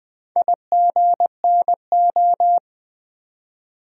Morse code, heard as a keyed tone, about 20 wpm, 700 Hz. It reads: IGNO
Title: Build A Word - 2 Letters At A Time - 6-Letter Words - 20wpm